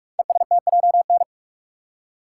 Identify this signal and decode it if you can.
Morse code, heard as a keyed tone, about 44 wpm, 700 Hz. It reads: ESTJN
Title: Code Groups - 5 Letter:Numbers - 45wpm